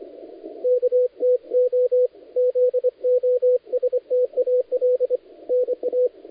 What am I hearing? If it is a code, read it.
Morse code, heard as a keyed tone, about 23 wpm, 500 Hz, with radio noise. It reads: KTO ZOSTAL NA
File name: dlazaawansowanych / nagraniazpasma / titawaNETcontrol